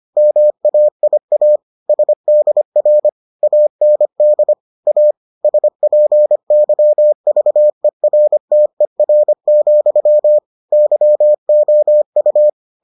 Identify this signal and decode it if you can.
Morse code, heard as a keyed tone, about 25 wpm, 600 Hz. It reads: MAIA SDR AND A SPYVERTER, YOU